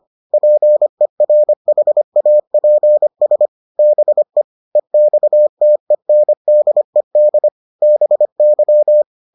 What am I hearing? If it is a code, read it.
Morse code, heard as a keyed tone, about 25 wpm, 600 Hz. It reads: PERHAPS BE EXTENDED BY